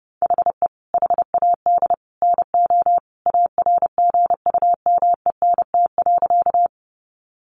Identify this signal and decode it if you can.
Morse code, heard as a keyed tone, about 30 wpm, 700 Hz. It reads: HE HAD NO ARGUMENT.
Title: Sentences from Top 1400 Words 30wpm